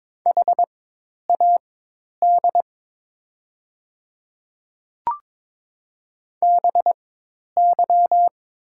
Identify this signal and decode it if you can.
Morse code, heard as a keyed tone, about 22 wpm, spaced as for 10 wpm, 700 Hz. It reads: HAD BY
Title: 4x Character Spacing - Top 100 Words - 22wpm